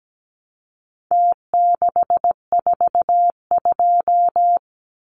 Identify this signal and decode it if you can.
Morse code, heard as a keyed tone, about 17 wpm, 700 Hz. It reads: T642
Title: Numbers - Contest Style - 4-Digit 17wpm